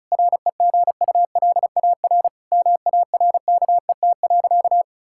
Morse code, heard as a keyed tone, about 34 wpm, 700 Hz. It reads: REGULAR MARKET.
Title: Sentences from Top 1000 Words - Review Part 6 - 35wpm